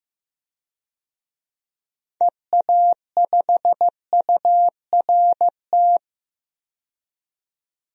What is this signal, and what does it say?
Morse code, heard as a keyed tone, about 15 wpm, 700 Hz. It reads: EA5URT
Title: DXCC-100 Call Signs - 6 Characters - 15wpm